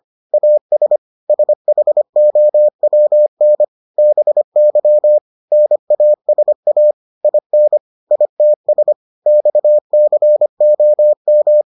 Morse code, heard as a keyed tone, about 25 wpm, 600 Hz. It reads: AS SHOWN BY NASA IN ITS XCOM